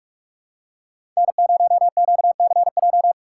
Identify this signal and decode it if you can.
Morse code, heard as a keyed tone, about 46 wpm, 700 Hz. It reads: N0QCJ
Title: USA Call Signs - 5 Characters - 3x Character Spacing - 45wpm